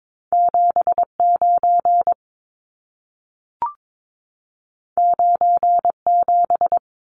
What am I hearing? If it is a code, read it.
Morse code, heard as a keyed tone, about 22 wpm, 700 Hz. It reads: 79 97
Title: Numbers - 2-Digit 22wpm